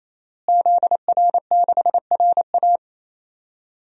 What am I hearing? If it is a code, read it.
Morse code, heard as a keyed tone, about 28 wpm, 700 Hz. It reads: ZR6RA
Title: DXCC-100 Call Signs - 5 Characters - Repeated 3x - 28wpm